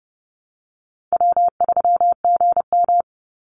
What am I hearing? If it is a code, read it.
Morse code, heard as a keyed tone, about 30 wpm, 700 Hz. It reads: W3GM
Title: USA Call Signs 30wpm